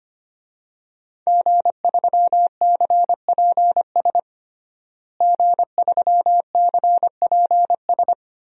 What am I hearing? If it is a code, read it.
Morse code, heard as a keyed tone, about 25 wpm, 700 Hz. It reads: G3CPS G3CPS
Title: DXCC-100 Call Signs - 5 Characters - Repeated 2x - 25wpm